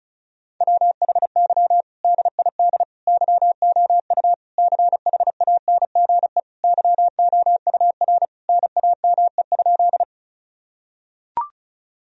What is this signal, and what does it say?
Morse code, heard as a keyed tone, about 34 wpm, 700 Hz. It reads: WHY DID YOU CHANGE YOUR NAME?
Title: Sentences from Top 1000 Words - Review Part 1 - 35wpm